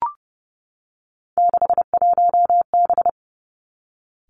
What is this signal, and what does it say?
Morse code, heard as a keyed tone, about 31 wpm, 700 Hz. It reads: 61B
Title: Three Challenging Letter-Number Combinations 30wpm